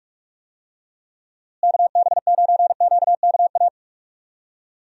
Morse code, heard as a keyed tone, about 46 wpm, 700 Hz. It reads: KB9QKA
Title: USA Call Signs - 6 Characters - Repeated 2x - 45wpm